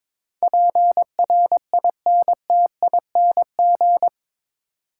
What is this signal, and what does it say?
Morse code, heard as a keyed tone, about 22 wpm, 700 Hz. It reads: PRINTING
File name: practice / 22wpm / Eight Letter Words - 22wpm